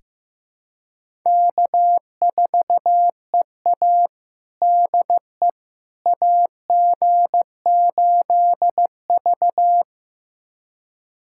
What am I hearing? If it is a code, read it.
Morse code, heard as a keyed tone, about 15 wpm, 700 Hz. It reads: K4EA DE AG8V
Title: USA Callsign DE Callsign - 4 Characters - 15wpm